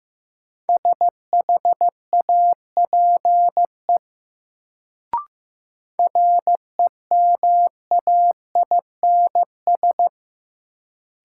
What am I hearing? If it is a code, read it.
Morse code, heard as a keyed tone, about 15 wpm, 700 Hz. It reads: SHAPE REMAINS
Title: Sets of 5 Words Encore - New Word Drill - 15wpm